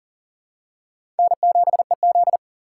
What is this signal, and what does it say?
Morse code, heard as a keyed tone, about 40 wpm, 700 Hz. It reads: N7EZ
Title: USA Call Signs - 4 Characters - 4x Character Spacing - 40wpm